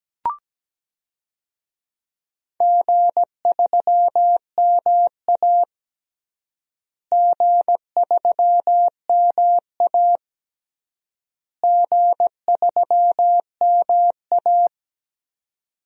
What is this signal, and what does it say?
Morse code, heard as a keyed tone, about 17 wpm, 700 Hz. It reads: G3MA G3MA G3MA
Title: DXCC-100 Call Signs - 4 Characters - Repeated 3x - 17wpm